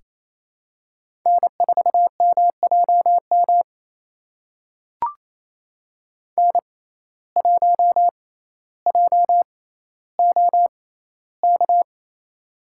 Morse code, heard as a keyed tone, about 28 wpm, spaced as for 14 wpm, 700 Hz. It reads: N4MJM N1JOK
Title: USA Call Signs - 5 Characters - 6x Character Spacing - 28wpm